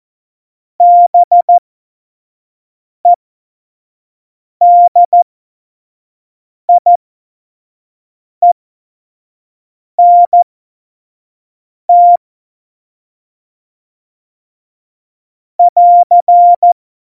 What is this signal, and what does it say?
Morse code, heard as a keyed tone, about 14 wpm, spaced as for 5 wpm, 700 Hz. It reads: BEDIENT <AR>
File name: Morsekurs / Witze / Witz 12. November 2019, 25 F70